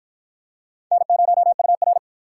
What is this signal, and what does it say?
Morse code, heard as a keyed tone, about 53 wpm, 700 Hz. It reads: N0UR